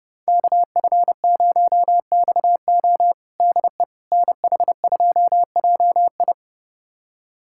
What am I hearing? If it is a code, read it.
Morse code, heard as a keyed tone, about 30 wpm, 700 Hz. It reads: KF0XO DE NH2JI
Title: USA Callsign DE Callsign - 5 Characters - 30wpm